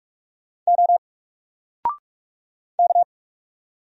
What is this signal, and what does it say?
Morse code, heard as a keyed tone, about 44 wpm, 700 Hz. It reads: O K
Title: Single Letter-Number Alt 45wpm